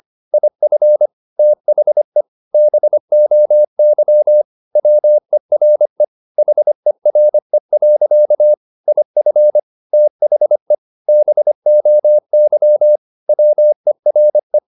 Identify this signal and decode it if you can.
Morse code, heard as a keyed tone, about 25 wpm, 600 Hz. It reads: IF THE BOY WERE HERE. IF THE BOY WERE